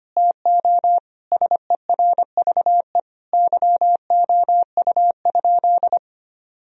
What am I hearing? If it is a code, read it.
Morse code, heard as a keyed tone, about 25 wpm, 700 Hz. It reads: TO SERVE YOU?
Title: Sentences from Top 1000 Words - Review Part 2 - 25wpm